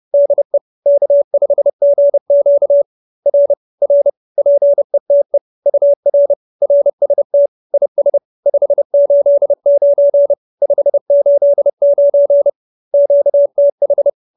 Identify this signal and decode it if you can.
Morse code, heard as a keyed tone, about 30 wpm, 570 Hz. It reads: DE K5GQ R R PETE UR RST IS 589 589 QTH
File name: QSO308 30